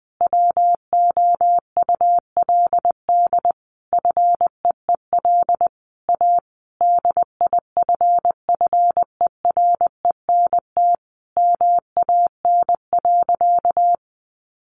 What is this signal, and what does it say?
Morse code, heard as a keyed tone, about 20 wpm, 700 Hz. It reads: WOULD FEEL A DIFFERENT MAN.
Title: Sentences from Top 600 Words 20wpm